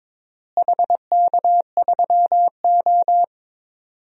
Morse code, heard as a keyed tone, about 22 wpm, 700 Hz. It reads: HK3O
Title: DXCC-100 Call Signs - 4 Characters - Repeated 2x - 22wpm